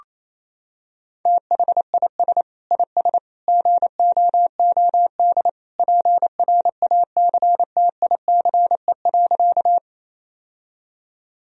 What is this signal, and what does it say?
Morse code, heard as a keyed tone, about 28 wpm, 700 Hz. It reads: THIS IS GOOD PRACTICE.
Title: Sentences from Top 1000 Words - Review Part 4 - 28wpm